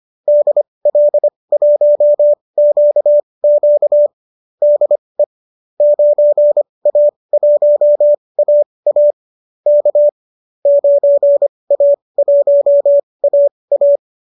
Morse code, heard as a keyed tone, about 25 wpm, spaced as for 20 wpm, 590 Hz. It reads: DL1QQ DE 9A1AA K 9A1AA
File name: qso219 20